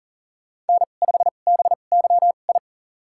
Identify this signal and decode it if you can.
Morse code, heard as a keyed tone, about 40 wpm, spaced as for 29 wpm, 700 Hz. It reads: N5BYI
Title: USA Call Signs - 5 Characters - 2x Character Spacing - 40wpm